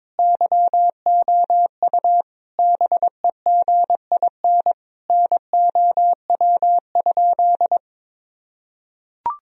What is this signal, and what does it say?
Morse code, heard as a keyed tone, about 22 wpm, 700 Hz. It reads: YOU BEGIN NOW?
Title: Sentences from Top 1000 Words - Review Part 1 - 22wpm